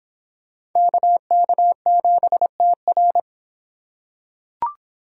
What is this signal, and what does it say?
Morse code, heard as a keyed tone, about 26 wpm, 700 Hz. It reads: KK7TR